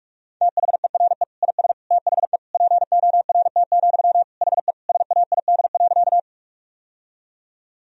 Morse code, heard as a keyed tone, about 41 wpm, 700 Hz. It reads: THERE IS THE PORT, HE SAID.
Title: Sentences from Top 1000 Words - Review Part 1 - 45wpm